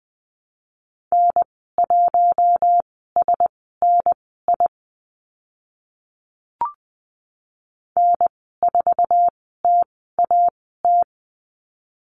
Morse code, heard as a keyed tone, about 20 wpm, spaced as for 15 wpm, 700 Hz. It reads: N1SNI N4TAT
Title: 048 Lesson 9g - Number 4 - Call Signs 20wpm